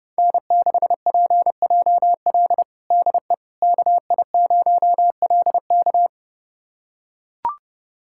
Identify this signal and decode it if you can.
Morse code, heard as a keyed tone, about 32 wpm, 700 Hz. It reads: N6PJL DE KI0LK
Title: USA Callsign DE Callsign - 5 Characters - 30wpm